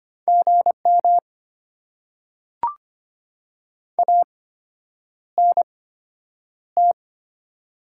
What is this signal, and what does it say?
Morse code, heard as a keyed tone, about 25 wpm, spaced as for 9 wpm, 700 Hz. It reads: GM ANT
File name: Top 90 QSO Elements - 8x Character Spacing - 25wpm